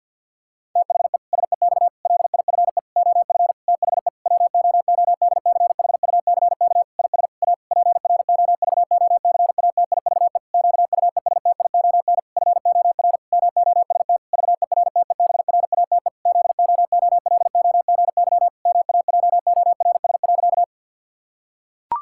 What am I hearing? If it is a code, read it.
Morse code, heard as a keyed tone, about 50 wpm, 700 Hz. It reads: THE SEX LIFE OF THE WOODCHUCK IS A PROVOCATIVE QUESTION FOR MOST VERTEBRATE ZOOLOGY MAJORS.
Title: Pangram Challenge v2 - 50wpm